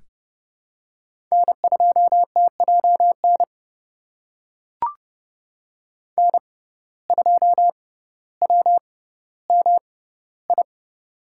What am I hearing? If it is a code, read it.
Morse code, heard as a keyed tone, about 31 wpm, spaced as for 16 wpm, 700 Hz. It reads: N2TJN N2WMI
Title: USA Call Signs - 5 Characters - 6x Character Spacing - 30wpm